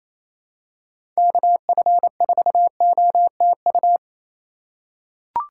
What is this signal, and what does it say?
Morse code, heard as a keyed tone, about 28 wpm, 700 Hz. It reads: KF4OTU
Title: USA Call Signs - 6 Characters - 3x Character Spacing - 28wpm